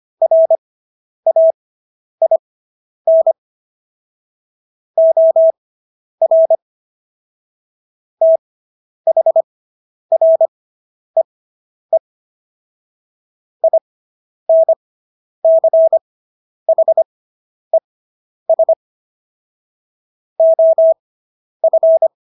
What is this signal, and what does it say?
Morse code, heard as a keyed tone, about 25 wpm, spaced as for 10 wpm, 650 Hz. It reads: RAIN OR THREE INCHES OF